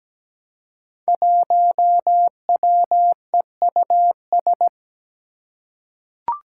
Morse code, heard as a keyed tone, about 17 wpm, 700 Hz. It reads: 1WEUS
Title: Five Challenging Letter-Number Combinations 17wpm